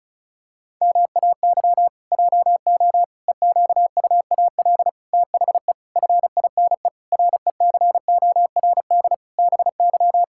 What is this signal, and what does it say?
Morse code, heard as a keyed tone, about 36 wpm, 700 Hz. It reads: MAY JO EQUAL THE FINE RECORD BY
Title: Pangram Challenge v2 - 35wpm